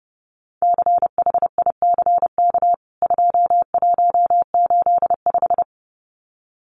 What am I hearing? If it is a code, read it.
Morse code, heard as a keyed tone, about 30 wpm, 700 Hz. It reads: CHICK 2185
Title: CWT Exchange 30wpm